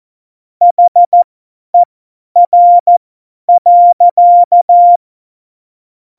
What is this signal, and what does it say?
Morse code, heard as a keyed tone, about 14 wpm, spaced as for 10 wpm, 700 Hz. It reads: HER.